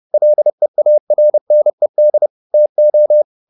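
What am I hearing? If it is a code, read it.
Morse code, heard as a keyed tone, about 30 wpm, 600 Hz. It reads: LEARNED TO
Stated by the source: Morse code at 30 wpm, the novel Never by Ken Follett in English, from CW med Gnister.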